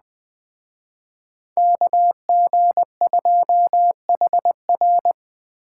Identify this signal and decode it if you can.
Morse code, heard as a keyed tone, about 20 wpm, 700 Hz. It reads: KG2HR